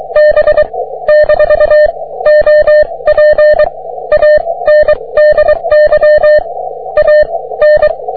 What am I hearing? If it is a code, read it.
Morse code, heard as a keyed tone, about 23 wpm, spaced as for 19 wpm, 600 Hz. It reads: B - OP ANDY A N
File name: wp / js6cqy 30m cw